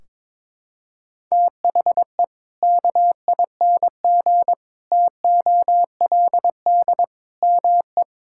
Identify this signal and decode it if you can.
Morse code, heard as a keyed tone, about 21 wpm, 700 Hz. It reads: THE KING TOLD ME
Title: Sentences from Top 1600 Words 22wpm